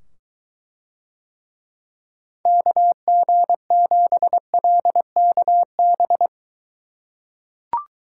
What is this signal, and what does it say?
Morse code, heard as a keyed tone, about 23 wpm, 700 Hz. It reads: KG7LKB